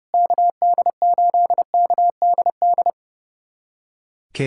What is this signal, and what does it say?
Morse code, heard as a keyed tone, about 30 wpm, 700 Hz. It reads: KD8KDD
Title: USA Callsign DE Callsign 30wpm